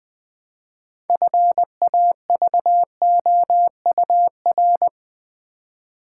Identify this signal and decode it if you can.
Morse code, heard as a keyed tone, about 20 wpm, 700 Hz. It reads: FAVOUR
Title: Build A Word - 2 Letters At A Time - 6-Letter Words - 20wpm